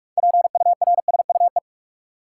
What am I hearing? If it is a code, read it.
Morse code, heard as a keyed tone, about 44 wpm, 700 Hz. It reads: PURSUE
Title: Six Letter Words - 45wpm